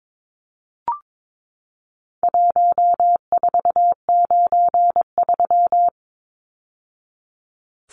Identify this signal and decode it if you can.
Morse code, heard as a keyed tone, about 22 wpm, 700 Hz. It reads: 1493
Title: Numbers - Contest Style - 4-Digit 22wpm